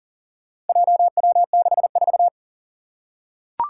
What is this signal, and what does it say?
Morse code, heard as a keyed tone, about 38 wpm, 700 Hz. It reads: JW64
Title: Four Challenging Letter-Number Combinations 40wpm